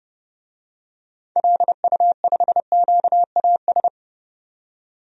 Morse code, heard as a keyed tone, about 32 wpm, 700 Hz. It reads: LU5QAS
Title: DXCC-100 Call Signs - 6 Characters - 30wpm